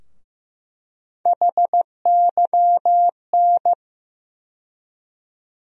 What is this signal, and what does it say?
Morse code, heard as a keyed tone, about 16 wpm, 700 Hz. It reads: HYN
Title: USA Call Sign Postfixes - 3 Letters 15wpm